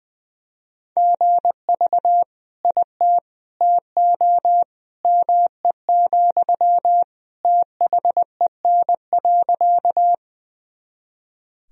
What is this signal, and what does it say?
Morse code, heard as a keyed tone, about 21 wpm, 700 Hz. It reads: GV IT TO ME, THEN.
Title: Sentences from Top 200 Words and Abbreviations 20wpm